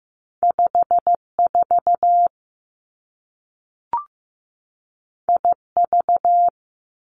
Morse code, heard as a keyed tone, about 15 wpm, 700 Hz. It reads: 5 4 I V
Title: Two Challenging Letter-Number Combinations 15wpm